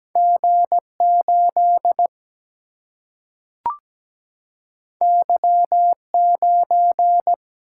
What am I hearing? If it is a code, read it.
Morse code, heard as a keyed tone, about 17 wpm, 700 Hz. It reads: G 8 Y 9
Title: Single-Letter Single-Number 17wpm